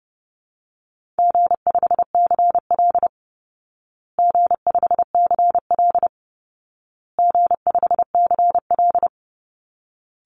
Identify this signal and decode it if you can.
Morse code, heard as a keyed tone, about 30 wpm, 700 Hz. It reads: G5CL G5CL G5CL
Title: Call Signs Repeated 2x - 30wpm